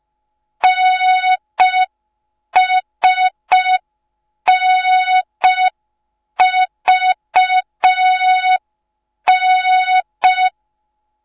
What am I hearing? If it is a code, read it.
Morse code, heard as a keyed tone, about 5 wpm, 3700 Hz, with noise. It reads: NSNVN